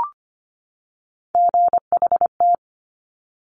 Morse code, heard as a keyed tone, about 25 wpm, 700 Hz. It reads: GHT
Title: Top Three Letter Combinations in Words 25wpm